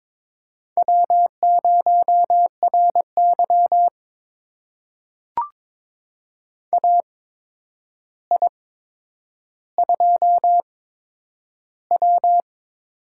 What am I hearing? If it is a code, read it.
Morse code, heard as a keyed tone, about 22 wpm, spaced as for 9 wpm, 700 Hz. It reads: W0RY AI2W
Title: USA Call Signs - 4 Characters - 8x Character Spacing - 22wpm